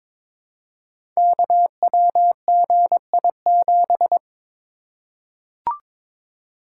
Morse code, heard as a keyed tone, about 22 wpm, 700 Hz. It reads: KWGI7
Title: Code Groups - 5 Letter:Numbers - 22wpm